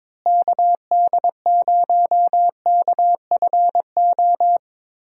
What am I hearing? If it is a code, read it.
Morse code, heard as a keyed tone, about 21 wpm, 700 Hz. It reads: KD0KFO
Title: USA Call Signs - 6 Characters - Repeated 3x - 22wpm